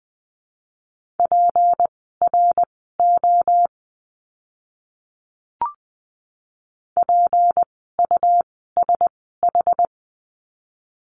Morse code, heard as a keyed tone, about 20 wpm, spaced as for 15 wpm, 700 Hz. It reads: PRO PUSH
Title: 153 Lesson 25f - Letter p - Words 20wpm